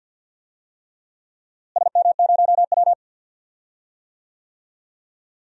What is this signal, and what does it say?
Morse code, heard as a keyed tone, about 45 wpm, 700 Hz. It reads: SM0W
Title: DXCC-100 Call Signs - 4 Characters - Repeated 3x - 50wpm